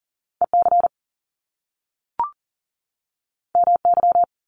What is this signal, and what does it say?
Morse code, heard as a keyed tone, about 38 wpm, 700 Hz. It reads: EC MY
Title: Top Two Letter Combinations in Words 40wpm